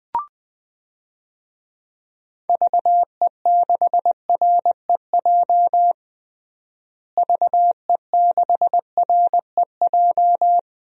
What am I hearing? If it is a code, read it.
Morse code, heard as a keyed tone, about 20 wpm, 700 Hz. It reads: VE6REJ VE6REJ
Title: DXCC-100 Call Signs - 6 Characters - Repeated 3x - 20wpm